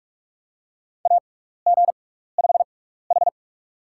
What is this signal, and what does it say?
Morse code, heard as a keyed tone, about 42 wpm, spaced as for 15 wpm, 700 Hz. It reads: AG5H
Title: USA Call Signs - 6x Character Spacing - 45wpm